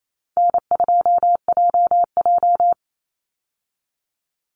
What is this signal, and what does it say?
Morse code, heard as a keyed tone, about 28 wpm, 700 Hz. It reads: N2JJ
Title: Easy Call Signs Repeated 2x - 28wpm